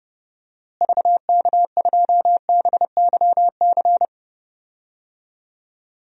Morse code, heard as a keyed tone, about 32 wpm, 700 Hz. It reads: VK2BYC
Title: DXCC-100 Call Signs - 6 Characters - Repeated 3x - 30wpm